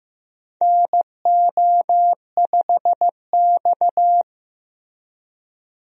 Morse code, heard as a keyed tone, about 15 wpm, 700 Hz. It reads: NO5X
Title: USA Call Signs - 4 Characters - Repeated 2x - 15wpm